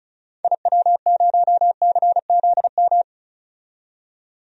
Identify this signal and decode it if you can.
Morse code, heard as a keyed tone, about 34 wpm, 700 Hz. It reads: IW0CZM
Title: DXCC-100 Call Signs - 6 Characters - 35wpm